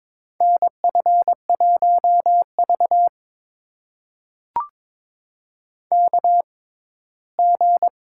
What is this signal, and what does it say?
Morse code, heard as a keyed tone, about 22 wpm, spaced as for 15 wpm, 700 Hz. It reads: NF1V KG
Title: USA Call Signs - 6x Character Spacing - 22wpm